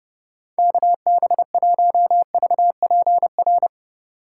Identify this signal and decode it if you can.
Morse code, heard as a keyed tone, about 31 wpm, 700 Hz. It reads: KB1VPR